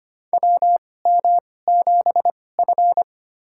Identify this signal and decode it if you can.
Morse code, heard as a keyed tone, about 25 wpm, spaced as for 18 wpm, 700 Hz. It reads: WM7F
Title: USA Call Signs - 4 Characters - 2x Character Spacing - 25wpm